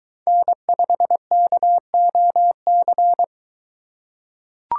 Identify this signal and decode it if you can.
Morse code, heard as a keyed tone, about 23 wpm, 700 Hz. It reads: N5KOC